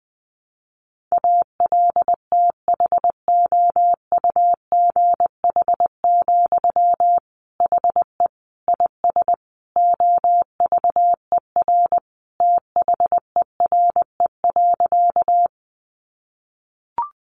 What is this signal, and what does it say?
Morse code, heard as a keyed tone, about 20 wpm, 700 Hz. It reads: ALTHOUGH, HE IS OVER THERE.
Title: Sentences from Top 600 Words 20wpm